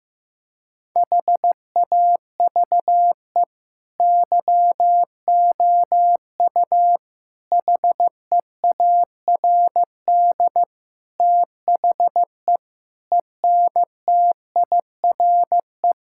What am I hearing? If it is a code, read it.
Morse code, heard as a keyed tone, about 15 wpm, 700 Hz. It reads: HAVE YOU HEARD THE ENTIRE
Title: Sentences from Top 1000 Words - Review Part 5 - 15wpm